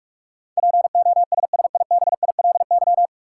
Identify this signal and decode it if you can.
Morse code, heard as a keyed tone, about 45 wpm, 700 Hz. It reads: POSSIBILY
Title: Sentences from Top 1300 Words - New Word Drill Alt - 45wpm